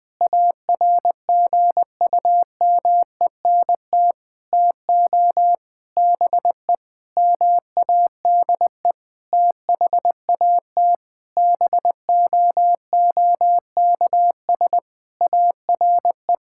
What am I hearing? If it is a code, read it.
Morse code, heard as a keyed tone, about 20 wpm, 700 Hz. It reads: ARGUMENT TO BE MADE THAT BOOKS ARE
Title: Sentences with 8-Word Phrases - Spoken Beginning - 20wpm